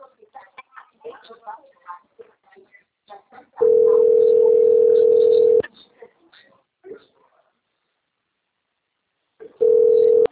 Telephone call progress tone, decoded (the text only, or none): ringback tone